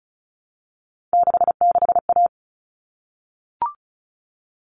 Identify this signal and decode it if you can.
Morse code, heard as a keyed tone, about 34 wpm, 700 Hz. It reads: 66A
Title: All Cut Numbers - 3-Digit 35wpm